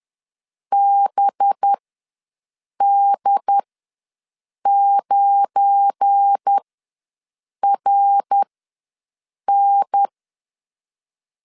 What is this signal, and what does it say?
Morse code, heard as a keyed tone, about 10 wpm, spaced as for 6 wpm, 800 Hz. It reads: BD9RN
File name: LEZIONI / 7 GF9Z / 7 GF9 ws 10wpm 4034